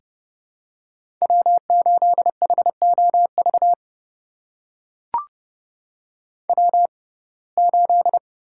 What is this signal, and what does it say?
Morse code, heard as a keyed tone, about 30 wpm, spaced as for 22 wpm, 700 Hz. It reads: W8HOV W8
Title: USA Call Signs - 6x Character Spacing - 30wpm